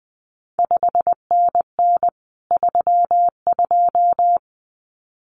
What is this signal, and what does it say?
Morse code, heard as a keyed tone, about 20 wpm, 700 Hz. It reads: 5NN 32
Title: CQ WW DX Contest Exchange Encore - 20wpm